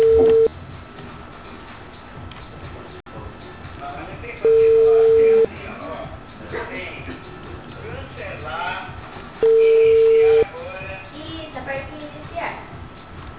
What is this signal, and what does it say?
Ringback tone (China, 450 Hz), with speech.